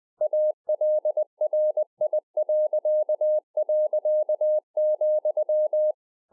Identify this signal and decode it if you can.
Morse code, heard as a keyed tone, about 20 wpm, 610 Hz. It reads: ALRI..,